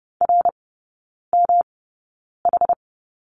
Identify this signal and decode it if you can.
Morse code, heard as a keyed tone, about 30 wpm, spaced as for 9 wpm, 700 Hz. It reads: RMH
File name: Lesson 18e - Letter m - Three Characters 30wpm